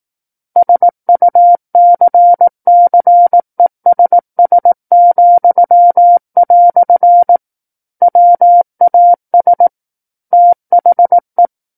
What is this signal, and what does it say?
Morse code, heard as a keyed tone, about 18 wpm, 700 Hz. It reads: SUCCESS," WAS THE
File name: Book One / 09-Chapter 9 - 18 WPM